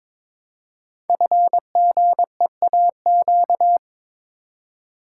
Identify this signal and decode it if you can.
Morse code, heard as a keyed tone, about 22 wpm, 690 Hz. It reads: FGEAQ
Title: Code Groups - 5 Letters - 22wpm